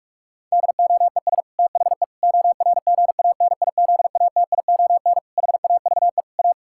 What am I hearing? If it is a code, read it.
Morse code, heard as a keyed tone, about 46 wpm, 700 Hz. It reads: DOES THE ORGANIZATION HAVE A
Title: Sentences from Top 1000 Words - Review Part 6 - 45wpm